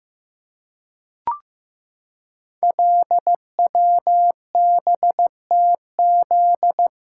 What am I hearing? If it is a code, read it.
Morse code, heard as a keyed tone, about 15 wpm, 700 Hz. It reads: LWBTZ
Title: Code Groups - 5 Letter:Numbers - 15wpm